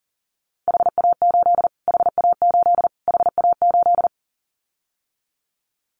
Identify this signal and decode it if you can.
Morse code, heard as a keyed tone, about 40 wpm, 700 Hz. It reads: HA8 HA8 HA8
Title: Callsign Prefixes for Top 25 DXCC - Repeated 3x - 40wpm